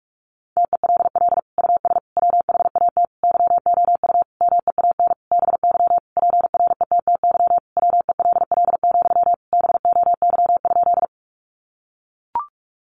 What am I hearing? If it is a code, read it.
Morse code, heard as a keyed tone, about 44 wpm, 700 Hz. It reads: TELL US WHAT YOU MEAN BY PRETTY WELL, BOY?
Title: Sentences from Top 1000 Words 45wpm